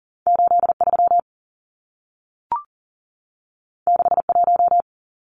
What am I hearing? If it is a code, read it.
Morse code, heard as a keyed tone, about 40 wpm, 700 Hz. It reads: 83 61
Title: Numbers - 2-Digit 40wpm